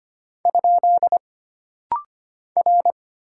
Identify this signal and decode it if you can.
Morse code, heard as a keyed tone, about 25 wpm, 700 Hz. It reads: ? R
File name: Single Character (Letter:Number:Punctuation:Prosign) - Rapid-Fire - 25wpm